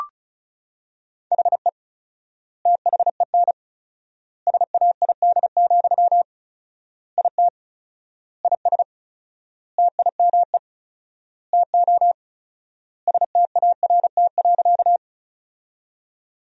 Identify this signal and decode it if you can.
Morse code, heard as a keyed tone, about 34 wpm, 700 Hz. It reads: HE THEN SAID, IT IS TIME TO START.
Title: 4x Word-Spacing - Sentences from Top 300 Words - 35wpm